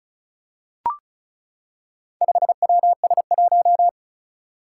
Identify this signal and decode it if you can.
Morse code, heard as a keyed tone, about 36 wpm, 700 Hz. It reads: 5WS1
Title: Four Challenging Letter-Number Combinations 35wpm